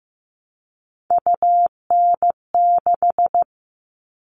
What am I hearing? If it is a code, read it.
Morse code, heard as a keyed tone, about 15 wpm, 700 Hz. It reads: UN6
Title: All Cut Numbers - 3-Digit 15wpm